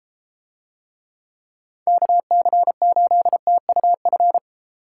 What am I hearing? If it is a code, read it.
Morse code, heard as a keyed tone, about 33 wpm, 700 Hz. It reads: KC8TUF